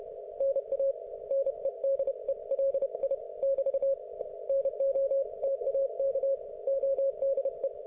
Morse code, heard as a keyed tone, about 31 wpm, 530 Hz. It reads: NA NEDELI <BT> NOTAK ODE